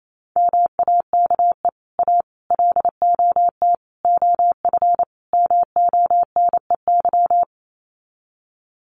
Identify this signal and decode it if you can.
Morse code, heard as a keyed tone, about 28 wpm, 700 Hz. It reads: MAKE A LOT OF MONEY
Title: Sets of 5 Words Encore2 28wpm